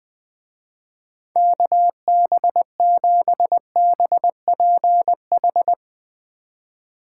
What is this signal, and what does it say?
Morse code, heard as a keyed tone, about 19 wpm, 700 Hz. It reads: KB7BPH
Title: Speed Racing - USA 2x3 Call Signs 20wpm